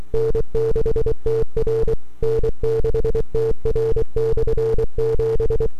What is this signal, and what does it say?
Morse code, heard as a keyed tone, about 24 wpm, 440 Hz. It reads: N6TR N6TR/7
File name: n6tr / 12Oct2009-160